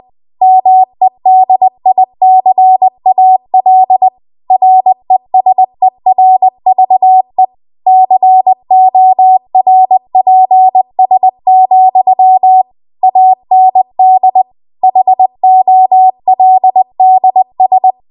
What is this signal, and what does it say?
Morse code, heard as a keyed tone, about 20 wpm, 750 Hz. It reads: MEDICAL RESERVE CORPS, AND HOLDS
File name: Morse / 20WPM